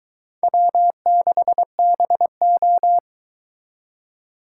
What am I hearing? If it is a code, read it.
Morse code, heard as a keyed tone, about 24 wpm, 700 Hz. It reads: W6BO